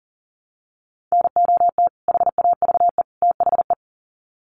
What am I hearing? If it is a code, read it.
Morse code, heard as a keyed tone, about 40 wpm, 700 Hz. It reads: NOT HAVE THE
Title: Sets of 3 Words 40wpm